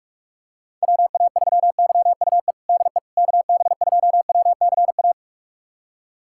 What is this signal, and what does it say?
Morse code, heard as a keyed tone, about 46 wpm, 700 Hz. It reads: WA3YUE DE KB2WCA
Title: USA Callsign DE Callsign 45wpm